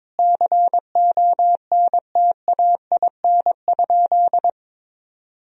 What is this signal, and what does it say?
Morse code, heard as a keyed tone, about 22 wpm, 700 Hz. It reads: CONTAIN?
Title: Sentences from Top 1000 Words - Review Part 2 - 22wpm